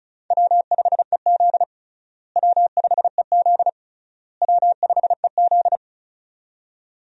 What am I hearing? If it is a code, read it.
Morse code, heard as a keyed tone, about 36 wpm, 700 Hz. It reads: W5EZ W5EZ W5EZ
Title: USA Call Signs - 4 Characters - Repeated 3x - 35wpm